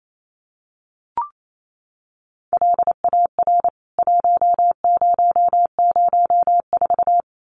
Morse code, heard as a keyed tone, about 28 wpm, 700 Hz. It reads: LAR 1004
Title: CWT Exchange 28wpm